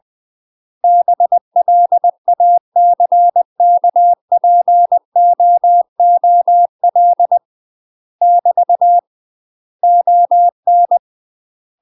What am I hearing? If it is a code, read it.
Morse code, heard as a keyed tone, about 20 wpm, 700 Hz. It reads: BLACKPOOL <BT> ON
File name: UK MASTER QSOs TRAINING / 20WPM / UK MASTER QSOs 0003